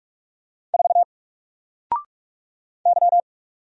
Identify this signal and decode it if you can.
Morse code, heard as a keyed tone, about 44 wpm, 700 Hz. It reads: 4 Y
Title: Single Character (Letter:Number:Punctuation:Prosign) Alt 45wpm